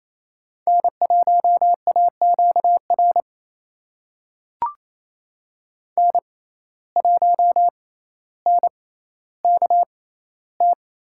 Morse code, heard as a keyed tone, about 28 wpm, spaced as for 14 wpm, 700 Hz. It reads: N1AQR N1NKT
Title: USA Call Signs - 5 Characters - 6x Character Spacing - 28wpm